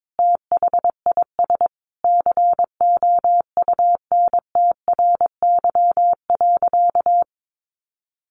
Morse code, heard as a keyed tone, about 22 wpm, 700 Hz. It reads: THIS COUNTRY.
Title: Sentences from Top 1400 Words 22wpm